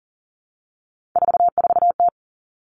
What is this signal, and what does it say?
Morse code, heard as a keyed tone, about 40 wpm, 700 Hz. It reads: 44T
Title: Three Challenging Letter-Number Combinations 40wpm